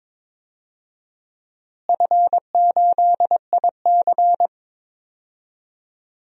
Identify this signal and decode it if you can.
Morse code, heard as a keyed tone, about 22 wpm, 700 Hz. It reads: F8IC
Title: DXCC-100 Call Signs - 4 Characters - Repeated 2x - 22wpm